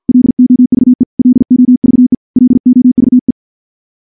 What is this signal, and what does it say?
Morse code, heard as a keyed tone, about 45 wpm, 250 Hz. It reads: LOVE LOVE LOVE
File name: morse / morse-training-50WPM